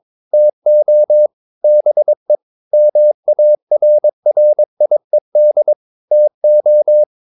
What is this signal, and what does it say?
Morse code, heard as a keyed tone, about 22 wpm, 600 Hz. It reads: TO BE MARRIED TO
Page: Romanen Never af Ken Follet på engelsk.